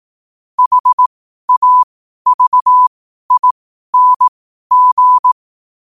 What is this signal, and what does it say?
Morse code, heard as a keyed tone, about 18 wpm, spaced as for 12 wpm, 1000 Hz. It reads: HAVING
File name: qotdfiles / 12wpm